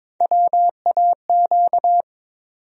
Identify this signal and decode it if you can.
Morse code, heard as a keyed tone, about 21 wpm, 700 Hz. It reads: WAQ
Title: USA Call Sign Postfixes - 3 Letters 22wpm